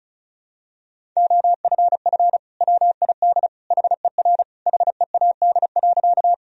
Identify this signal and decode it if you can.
Morse code, heard as a keyed tone, about 36 wpm, 700 Hz. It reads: OFF WID HER HEAD.
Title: Sentences from Top 300 Words and Abbreviations 35wpm